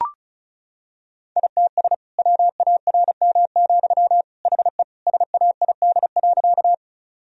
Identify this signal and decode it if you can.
Morse code, heard as a keyed tone, about 34 wpm, 700 Hz. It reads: ITS WARM, HE SAID.
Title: Sentences from Top 1000 Words - Review Part 2 - 35wpm